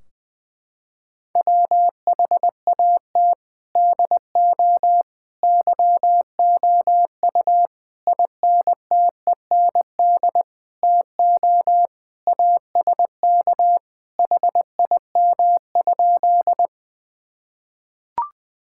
Morse code, heard as a keyed tone, about 21 wpm, 700 Hz. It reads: WHAT DO YOU INTEND TO ASK HIM?
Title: Sentences from Top 1600 Words 20wpm